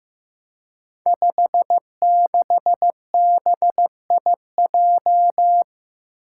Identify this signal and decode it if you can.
Morse code, heard as a keyed tone, about 15 wpm, 700 Hz. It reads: 56BIJ
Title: Five Challenging Letter-Number Combinations 15wpm